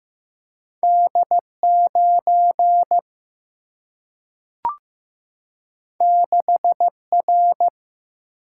Morse code, heard as a keyed tone, about 15 wpm, 700 Hz. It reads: D 9 6 R
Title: Single-Letter Single-Number 15wpm